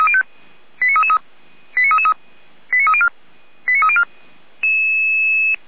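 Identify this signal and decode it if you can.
Five-tone selective call (ZVEI), heard as quick runs of five tones, tones of about 70 ms. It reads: 89333 89333 89304 89304
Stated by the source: ZVEI five-tone selective call received with an SDR, archived by Feuerwehr Orlamünde.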